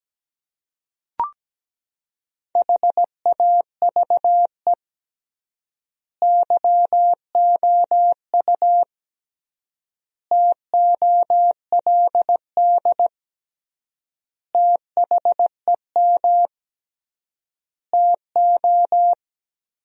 Morse code, heard as a keyed tone, about 17 wpm, 700 Hz. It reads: HAVE YOU TOLD THEM TO
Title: 3x Word-Spacing - Sentences from Top 500 Words - 17wpm